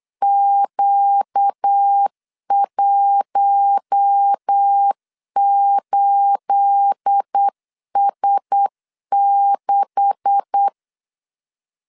Morse code, heard as a keyed tone, about 8 wpm, 800 Hz. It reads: Q18S6
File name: LEZIONI / 9 286 8wpm 3425